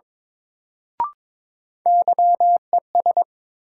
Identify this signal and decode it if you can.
Morse code, heard as a keyed tone, about 21 wpm, 700 Hz. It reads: YES
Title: Sentences from Top 500 Words - New Word Drill - 22wpm